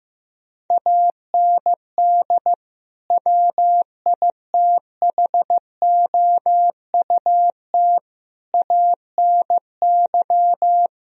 Morse code, heard as a keyed tone, about 15 wpm, 700 Hz. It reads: AND WITHOUT ANY
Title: Sentences from Top 1000 Words - Review Part 4 - 15wpm